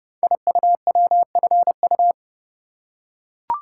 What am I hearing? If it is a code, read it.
Morse code, heard as a keyed tone, about 30 wpm, 700 Hz. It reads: IUWFU
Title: Code Groups - 5 Letters - 30wpm